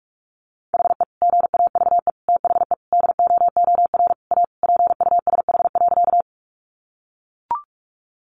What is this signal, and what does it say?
Morse code, heard as a keyed tone, about 44 wpm, 700 Hz. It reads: HE GAVE THE DOOR A PUSH.
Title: Sentences from Top 1300 Words 45wpm